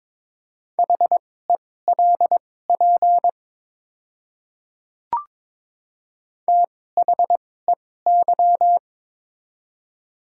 Morse code, heard as a keyed tone, about 22 wpm, spaced as for 16 wpm, 700 Hz. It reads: HELP THEY
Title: 2x Character Spacing - Top 300 Words - 22wpm